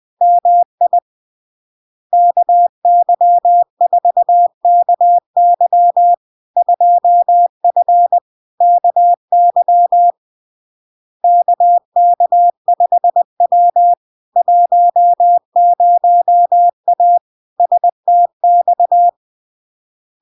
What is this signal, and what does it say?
Morse code, heard as a keyed tone, about 20 wpm, 700 Hz. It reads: MI KY4KY 2F KY KK5W 10A STX
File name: LARK-FD-2022-Log-in-CW-20wpm-700-Hz